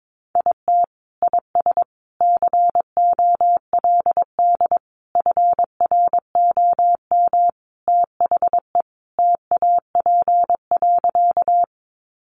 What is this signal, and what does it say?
Morse code, heard as a keyed tone, about 22 wpm, 700 Hz. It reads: IT IS COLD FROM THE TAP.
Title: Sentences from Top 1400 Words 22wpm